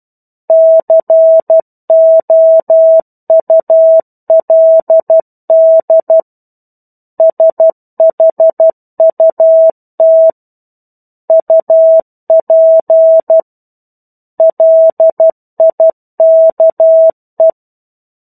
Morse code, heard as a keyed tone, about 12 wpm, 650 Hz. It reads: COULD SHUT UP LIKE
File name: 01-Chapter-01 12 WPM